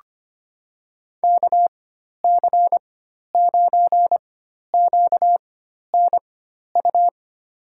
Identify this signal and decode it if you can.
Morse code, heard as a keyed tone, about 25 wpm, spaced as for 12 wpm, 700 Hz. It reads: KC9QNU
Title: USA Call Signs - 4x Character Spacing - 25wpm